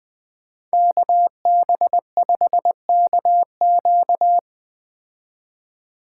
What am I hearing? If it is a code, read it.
Morse code, heard as a keyed tone, about 20 wpm, 700 Hz. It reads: KB5KQ
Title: Speed Racing - USA 2x2 Call Signs 17wpm